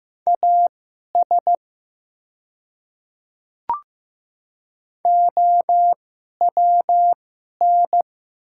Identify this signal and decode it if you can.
Morse code, heard as a keyed tone, about 15 wpm, spaced as for 11 wpm, 700 Hz. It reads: AS OWN
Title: 2x Character Spacing - Top 300 Words - 15wpm